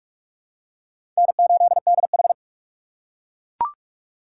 Morse code, heard as a keyed tone, about 41 wpm, 700 Hz. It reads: N8DH
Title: USA Call Signs - 4 Characters - 6x Character Spacing - 45wpm